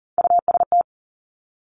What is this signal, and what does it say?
Morse code, heard as a keyed tone, about 40 wpm, 700 Hz. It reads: UST
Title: Top Three Letter Combinations in Words 40wpm